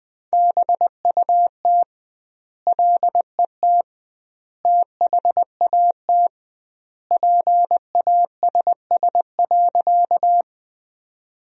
Morse code, heard as a keyed tone, about 20 wpm, 700 Hz. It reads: BUT LET THAT PASS.
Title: 2x Word-Spacing - Sentences from Top 400 Words - 20wpm